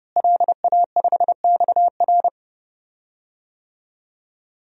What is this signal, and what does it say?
Morse code, heard as a keyed tone, about 30 wpm, 700 Hz. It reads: LA5XR
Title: DXCC-100 Call Signs - 5 Characters - Repeated 3x - 30wpm